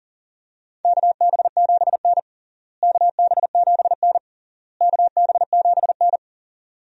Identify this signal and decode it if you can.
Morse code, heard as a keyed tone, about 40 wpm, 700 Hz. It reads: KB7N KB7N KB7N
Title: USA Call Signs - 4 Characters - Repeated 3x - 40wpm